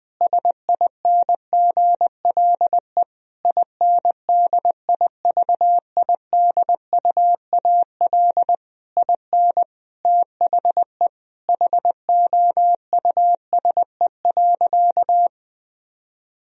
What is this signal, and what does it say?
Morse code, heard as a keyed tone, about 20 wpm, 700 Hz. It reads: SINGLE INDIVIDUAL IN THE HOUSE.
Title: Sentences from Top 1000 Words - Review Part 4 - 20wpm